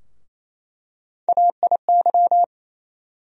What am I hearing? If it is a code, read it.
Morse code, heard as a keyed tone, about 28 wpm, 700 Hz. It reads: AIY
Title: USA Call Sign Postfixes - 3 Letters 28wpm